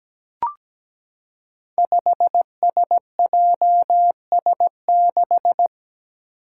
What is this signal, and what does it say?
Morse code, heard as a keyed tone, about 17 wpm, 700 Hz. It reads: 5SJS6
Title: Five Challenging Letter-Number Combinations 17wpm